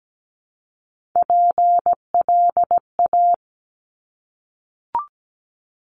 Morse code, heard as a keyed tone, about 17 wpm, 700 Hz. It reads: PLA
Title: Top Three Letter Combinations in Words 17wpm